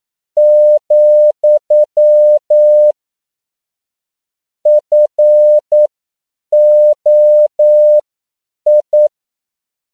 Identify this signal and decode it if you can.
Morse code, heard as a keyed tone, about 9 wpm, spaced as for 7 wpm, 600 Hz. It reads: , FOI
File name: CW / 10PPM